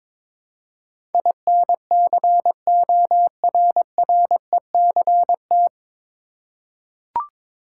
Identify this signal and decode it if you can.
Morse code, heard as a keyed tone, about 22 wpm, 700 Hz. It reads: INCORRECT
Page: Morse Code Ninja practice 22wpm